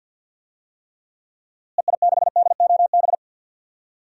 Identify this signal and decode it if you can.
Morse code, heard as a keyed tone, about 52 wpm, 700 Hz. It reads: EI6DOB
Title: DXCC-100 Call Signs - 6 Characters - Repeated 3x - 50wpm